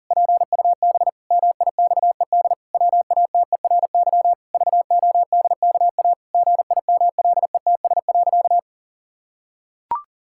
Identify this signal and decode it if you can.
Morse code, heard as a keyed tone, about 40 wpm, 700 Hz. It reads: PUB MIXED WATERY VODKA GIMLETS.
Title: Pangram Challenge v2 - 40wpm